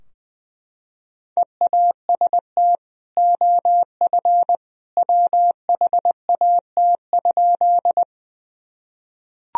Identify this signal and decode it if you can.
Morse code, heard as a keyed tone, about 20 wpm, 700 Hz. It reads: EAST OF WHAT?
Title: Sentences from Top 1000 Words - Review Part 3 - 20wpm